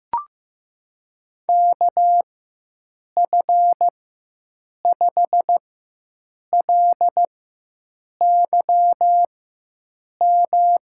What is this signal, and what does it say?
Morse code, heard as a keyed tone, about 15 wpm, spaced as for 7 wpm, 700 Hz. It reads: KF5LYM
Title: USA Call Signs - 6 Characters - 4x Character Spacing - 15wpm